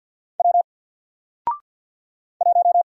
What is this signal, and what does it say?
Morse code, heard as a keyed tone, about 48 wpm, 700 Hz. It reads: W 1
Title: Single Character (Letter:Number:Punctuation:Prosign) Alt 50wpm